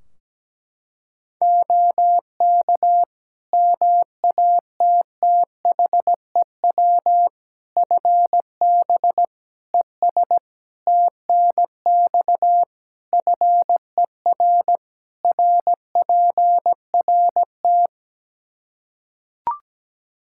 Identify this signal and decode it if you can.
Morse code, heard as a keyed tone, about 17 wpm, 700 Hz. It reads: OK MATTHEW FB ES TNX FER RPRT
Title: Practice QSO - Exchange 2 - Info - 17wpm